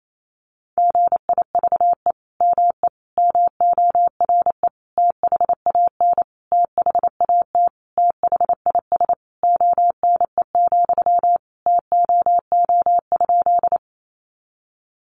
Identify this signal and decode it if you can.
Morse code, heard as a keyed tone, about 28 wpm, 700 Hz. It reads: GIVE ME MORE THAN THAT THIS ONE, TOO?
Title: Sentences from Top 200 Words 28wpm